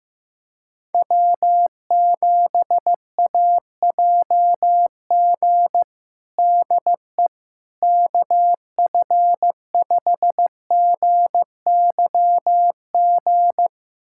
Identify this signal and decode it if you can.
Morse code, heard as a keyed tone, about 15 wpm, 700 Hz. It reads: W7AJG DE KF5GYG
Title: USA Callsign DE Callsign 15wpm